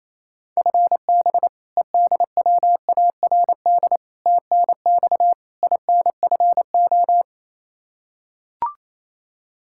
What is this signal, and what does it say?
Morse code, heard as a keyed tone, about 28 wpm, 700 Hz. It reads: FB EDWARD TNX INFO
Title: Practice QSO - Exchange 3 - More Info - 28wpm